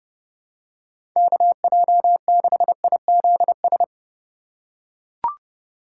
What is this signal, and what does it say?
Morse code, heard as a keyed tone, about 30 wpm, 700 Hz. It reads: KJ6IZS
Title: USA Call Signs - 6 Characters - 3x Character Spacing - 30wpm